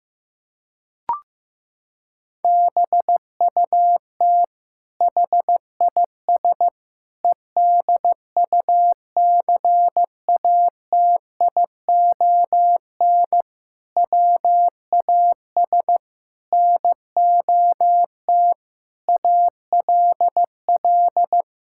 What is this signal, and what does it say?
Morse code, heard as a keyed tone, about 15 wpm, 700 Hz. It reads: BUT HIS EDUCATION WAS NOT ALL
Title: Sentences from Top 1000 Words - Review Part 6 - 15wpm